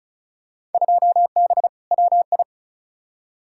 Morse code, heard as a keyed tone, about 36 wpm, 700 Hz. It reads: 2B WI
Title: Field Day Exchanges - 35wpm